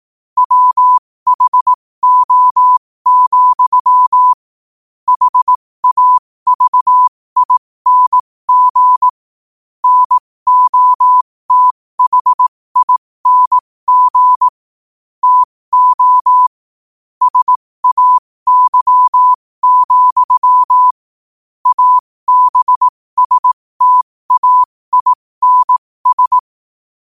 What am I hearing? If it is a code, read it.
Morse code, heard as a keyed tone, about 18 wpm, spaced as for 15 wpm, 1000 Hz. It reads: WHO, HAVING NOTHING TO SAY, ABSTAINS
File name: qotdfiles / 15wpm